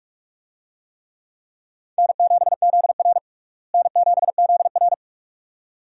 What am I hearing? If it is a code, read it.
Morse code, heard as a keyed tone, about 46 wpm, 700 Hz. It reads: N7ZR N7ZR
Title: USA Call Signs - 4 Characters - Repeated 2x - 45wpm